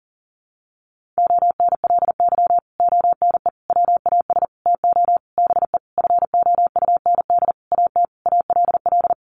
Morse code, heard as a keyed tone, about 40 wpm, 700 Hz. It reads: ONLY ONE WAS TO BE FOUND AT ALL
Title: Sentences from Top 1400 Words 40wpm